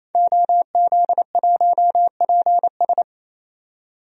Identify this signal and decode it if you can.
Morse code, heard as a keyed tone, about 27 wpm, 700 Hz. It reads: OZ1PS
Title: DXCC-100 Call Signs - 5 Characters - 28wpm